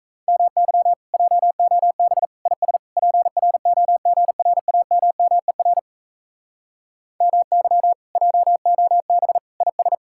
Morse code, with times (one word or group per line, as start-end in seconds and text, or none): 0.3-0.9 MY
1.1-2.2 JOB
2.5-2.8 IS
3.0-5.8 PROGRAMMER
7.2-7.9 MY
8.2-9.4 JOB
9.6-9.9 IS